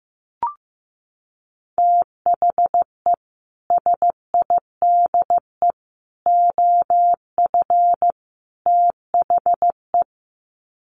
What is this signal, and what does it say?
Morse code, heard as a keyed tone, about 14 wpm, 700 Hz. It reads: THE SIDE OF THE
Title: Sets of 4 Words 15wpm